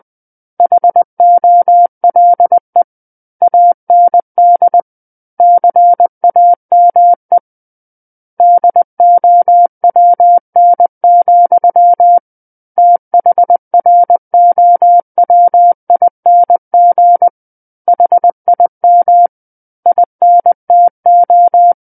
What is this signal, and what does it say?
Morse code, heard as a keyed tone, about 20 wpm, 700 Hz. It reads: HOLE AND CAME DOWN, THROWING HIM INTO
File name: 20 WPM